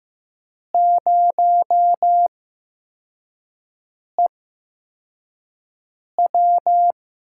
Morse code, heard as a keyed tone, about 15 wpm, 700 Hz. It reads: 0 E W
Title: USA Call Signs - 5 Characters - 8x Character Spacing - 15wpm